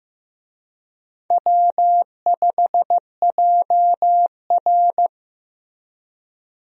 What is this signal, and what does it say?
Morse code, heard as a keyed tone, about 15 wpm, 700 Hz. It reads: W5JR
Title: USA Call Signs - 4 Characters - 15wpm